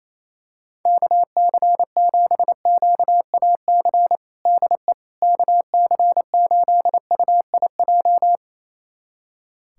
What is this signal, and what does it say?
Morse code, heard as a keyed tone, about 28 wpm, 700 Hz. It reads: KC7QAC DE KC8UIJ
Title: USA Callsign DE Callsign - 6 Characters - 28wpm